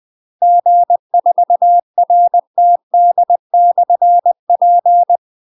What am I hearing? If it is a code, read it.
Morse code, heard as a keyed tone, about 20 wpm, 700 Hz. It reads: G4RTD/P
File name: UK MASTER QSOs TRAINING / 20WPM / UK MASTER QSOs 0049